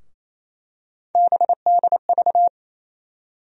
Morse code, heard as a keyed tone, about 28 wpm, 700 Hz. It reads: BDV
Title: USA Call Sign Postfixes - 3 Letters 28wpm